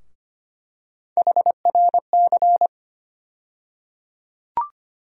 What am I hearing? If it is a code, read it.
Morse code, heard as a keyed tone, about 25 wpm, 700 Hz. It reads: HRC
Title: USA Call Sign Postfixes - 3 Letters 25wpm